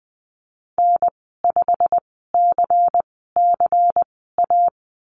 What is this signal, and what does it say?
Morse code, heard as a keyed tone, about 20 wpm, spaced as for 14 wpm, 700 Hz. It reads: N5CCA
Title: 231 Lesson 37g - Letter x - Call Signs 20wpm